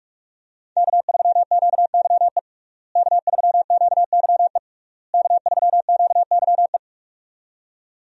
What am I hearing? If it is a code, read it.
Morse code, heard as a keyed tone, about 46 wpm, 700 Hz. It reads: K3QYE K3QYE K3QYE
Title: USA Call Signs - 5 Characters - Repeated 3x - 45wpm